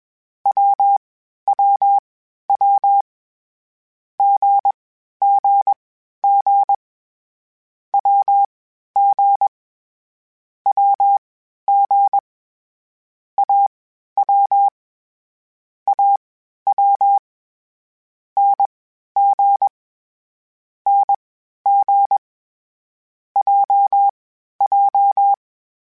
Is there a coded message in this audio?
Yes: Morse code from 0.5 to 25.3 s, heard as a keyed tone, about 21 wpm, spaced as for 12 wpm, 800 Hz. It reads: WWW GGG WG WG AW AW NG NG JJ